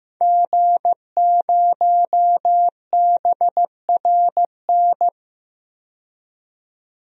Morse code, heard as a keyed tone, about 15 wpm, 700 Hz. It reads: G0BRN
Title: DXCC-100 Call Signs - 5 Characters - 15wpm